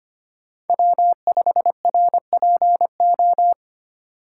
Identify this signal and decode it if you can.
Morse code, heard as a keyed tone, about 25 wpm, 700 Hz. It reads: W5RPO
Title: USA Call Signs - 5 Characters - 2x Character Spacing - 25wpm